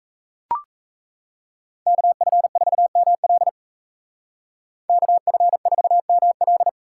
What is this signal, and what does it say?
Morse code, heard as a keyed tone, about 37 wpm, 700 Hz. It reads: KF4ML KF4ML